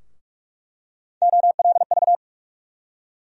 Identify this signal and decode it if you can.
Morse code, heard as a keyed tone, about 46 wpm, 700 Hz. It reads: OLV
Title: USA Call Sign Postfixes - 3 Letters 45wpm